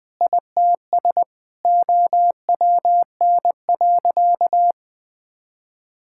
Morse code, heard as a keyed tone, about 20 wpm, 700 Hz. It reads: ITS OWN.
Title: Sentences with 8-Word Phrases - 20wpm